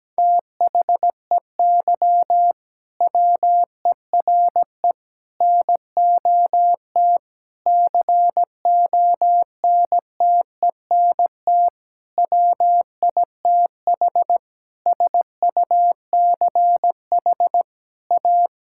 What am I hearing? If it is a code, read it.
Morse code, heard as a keyed tone, about 17 wpm, 700 Hz. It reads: THEY WERE NOT CONTENT WITH SUCH A
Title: Sentences from Top 1000 Words - Review Part 6 - 17wpm